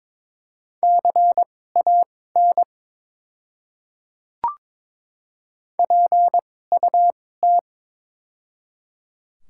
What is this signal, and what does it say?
Morse code, heard as a keyed tone, about 22 wpm, spaced as for 16 wpm, 690 Hz. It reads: CAN PUT
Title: 2x Character Spacing - Top 200 Words - 22wpm